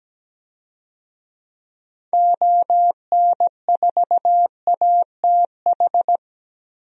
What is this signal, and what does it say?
Morse code, heard as a keyed tone, about 17 wpm, 700 Hz. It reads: ON4ATH
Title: DXCC-100 Call Signs - 6 Characters - Repeated 2x - 17wpm